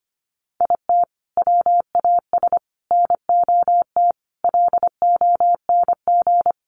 Morse code, heard as a keyed tone, about 25 wpm, 700 Hz. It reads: IT WAS NOT LONG
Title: Sets of 5 Words Encore2 25wpm